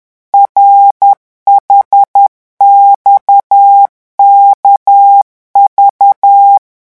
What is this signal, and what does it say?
Morse code, heard as a keyed tone, about 10 wpm, 800 Hz. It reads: RHXKV
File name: morse mp3 / revision5-vitesse 10 mots